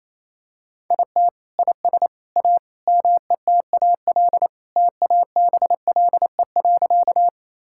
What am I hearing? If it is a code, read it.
Morse code, heard as a keyed tone, about 28 wpm, 700 Hz. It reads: IT IS A METAL TABLE.
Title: Sentences from Top 1000 Words - Review Part 6 - 28wpm